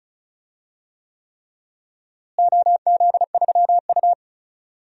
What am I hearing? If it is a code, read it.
Morse code, heard as a keyed tone, about 34 wpm, 700 Hz. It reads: OZ3U
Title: DXCC-100 Call Signs - 4 Characters - Repeated 3x - 35wpm